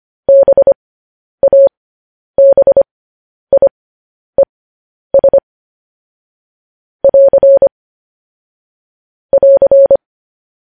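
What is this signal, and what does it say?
Morse code, heard as a keyed tone, about 25 wpm, spaced as for 10 wpm, 550 Hz. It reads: BABIES <AR> <AR>
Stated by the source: Morse code news headlines sent as a reading transcription, preceded by vvv.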